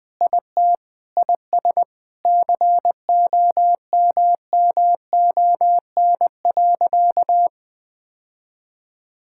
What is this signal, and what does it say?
Morse code, heard as a keyed tone, about 20 wpm, 700 Hz. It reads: IT IS COMMON.
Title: Sentences with 4-Word Phrases - 20wpm